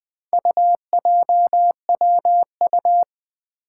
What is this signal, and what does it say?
Morse code, heard as a keyed tone, about 20 wpm, 700 Hz. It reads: UJWU
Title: Four Challenging Letter-Number Combinations 20wpm